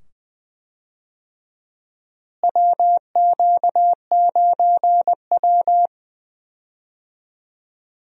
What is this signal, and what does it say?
Morse code, heard as a keyed tone, about 20 wpm, 700 Hz. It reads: WQ9W
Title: USA Call Signs - 4 Characters - Repeated 2x - 20wpm